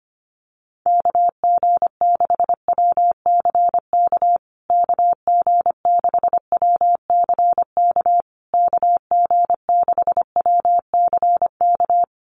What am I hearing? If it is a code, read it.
Morse code, heard as a keyed tone, about 25 wpm, 700 Hz. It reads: KG6WCK KG6WCK KG6WCK
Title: USA Call Signs Repeated 3x 25wpm